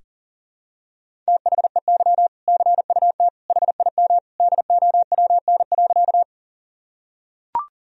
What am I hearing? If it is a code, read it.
Morse code, heard as a keyed tone, about 40 wpm, 700 Hz. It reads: THEY CUT HIM DOWN.
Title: Sentences from Top 1000 Words - Review Part 1 - 40wpm